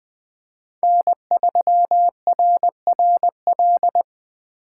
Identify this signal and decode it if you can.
Morse code, heard as a keyed tone, about 24 wpm, 700 Hz. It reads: N3RRL
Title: Speed Racing - USA 1x3 Call Signs 20wpm